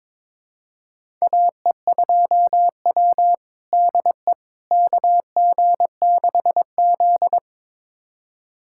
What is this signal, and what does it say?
Morse code, heard as a keyed tone, about 22 wpm, 700 Hz. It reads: AE2W DE KG6Z
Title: USA Callsign DE Callsign - 4 Characters - 22wpm